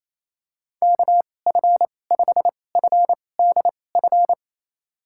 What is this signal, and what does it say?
Morse code, heard as a keyed tone, about 28 wpm, spaced as for 20 wpm, 700 Hz. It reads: KF5FDF
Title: USA Call Signs - 6 Characters - 2x Character Spacing - 28wpm